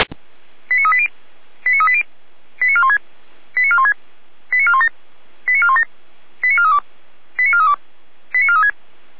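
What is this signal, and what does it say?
Five-tone selective call (ZVEI), heard as quick runs of five tones, tones of about 70 ms. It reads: 89380 89380 89416 89416 89417 89417 89432 89432 89436
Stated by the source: Decoded by SDR as ZVEI five-tone selective call.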